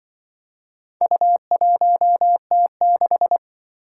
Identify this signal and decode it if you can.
Morse code, heard as a keyed tone, about 25 wpm, 700 Hz. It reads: U1T6
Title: Speed-Racing - Four Letter-Number Combinations - 20wpm